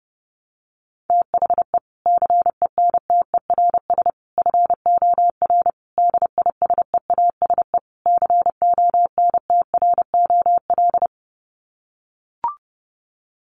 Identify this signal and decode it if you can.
Morse code, heard as a keyed tone, about 30 wpm, 700 Hz. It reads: THE CENTERS FOR DISEASE CONTROL
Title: Sets of 5 Words Encore2 30wpm